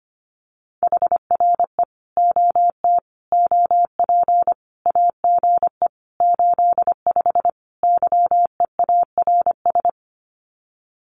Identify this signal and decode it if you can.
Morse code, heard as a keyed tone, about 25 wpm, 700 Hz. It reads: HRE OT OP AGE 85 YEARS
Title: Common QSO Phrases 25wpm